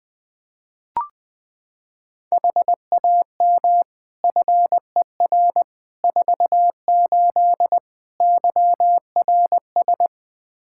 Morse code, heard as a keyed tone, about 21 wpm, 700 Hz. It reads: HAM FER 48 YRS
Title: K1USN SST QSO Practice 20wpm